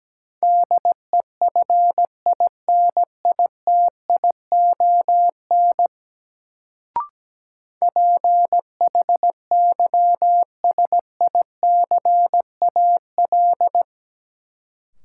Morse code, heard as a keyed tone, about 17 wpm, 700 Hz. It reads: DEFINITION PHYSICAL
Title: Sentences from Top 1000 Words - Review Part 4 - New Word Drill - 17wpm